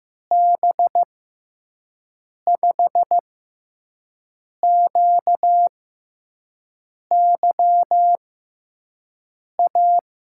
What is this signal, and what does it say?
Morse code, heard as a keyed tone, about 15 wpm, spaced as for 5 wpm, 700 Hz. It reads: B5QYA
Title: USA Call Signs - 6 Characters - 6x Character Spacing - 15wpm